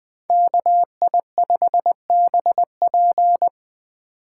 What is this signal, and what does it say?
Morse code, heard as a keyed tone, about 20 wpm, 700 Hz. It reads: KI5BP
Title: Speed Racing - USA 2x2 Call Signs 17wpm